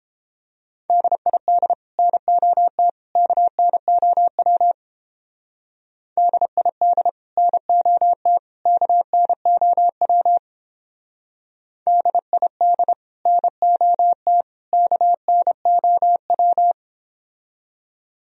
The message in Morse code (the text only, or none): DID NOT KNOW DID NOT KNOW DID NOT KNOW